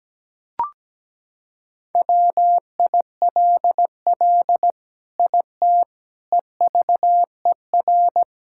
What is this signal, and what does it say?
Morse code, heard as a keyed tone, about 17 wpm, 700 Hz. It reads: WILL IT EVER
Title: Sentences from Top 1000 Words - Review Part 5 - 17wpm